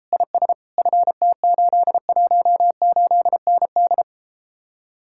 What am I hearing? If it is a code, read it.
Morse code, heard as a keyed tone, about 34 wpm, 700 Hz. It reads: IS FT818ND